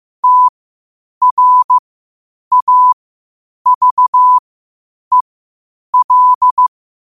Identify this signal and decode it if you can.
Morse code, heard as a keyed tone, about 15 wpm, spaced as for 8 wpm, 1000 Hz. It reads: TRAVEL
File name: qotdfiles / 8wpm